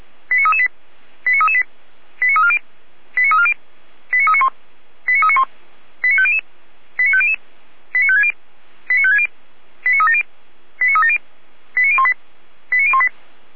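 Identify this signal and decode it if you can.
Five-tone selective call (ZVEI), heard as quick runs of five tones, tones of about 80 ms. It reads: 89338 89338 89340 89340 89391 89391 89500 89500 89560 89560 89370 89370 89018 89018
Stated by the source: ZVEI five-tone call received by SDR.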